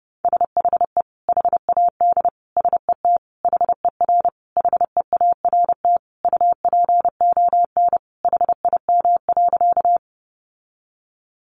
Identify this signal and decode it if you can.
Morse code, heard as a keyed tone, about 30 wpm, 700 Hz. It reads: SHE HAD SET HER HEART UPON HIM.
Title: Sentences from Top 1200 Words 30wpm